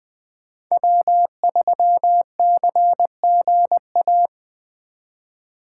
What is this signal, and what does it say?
Morse code, heard as a keyed tone, about 24 wpm, 700 Hz. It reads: W3CGA
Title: Speed Racing - USA 1x3 Call Signs 20wpm